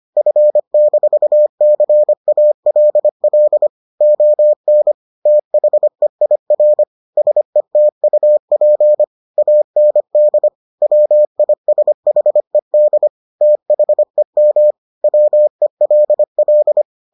Morse code, heard as a keyed tone, about 25 wpm, 600 Hz. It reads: F-CALL ON THEIR SETUP AND WISHED THEM WELL